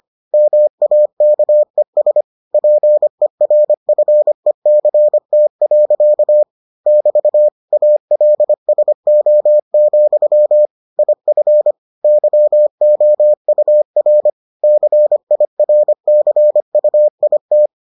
Morse code, heard as a keyed tone, about 25 wpm, 600 Hz. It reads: MAKES PERFECT. <BT> ALSO, IF YOUR CIRCUIT